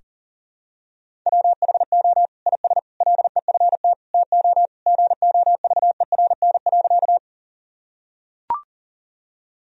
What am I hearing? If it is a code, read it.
Morse code, heard as a keyed tone, about 40 wpm, 700 Hz. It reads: WHO IS LEFT TO GOVERN.
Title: Sentences from Top 1000 Words - Review Part 2 - 40wpm